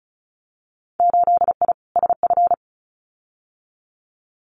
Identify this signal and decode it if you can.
Morse code, heard as a keyed tone, about 34 wpm, 700 Hz. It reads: 8I SF
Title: Winter Field Day Exchanges Encore 35wpm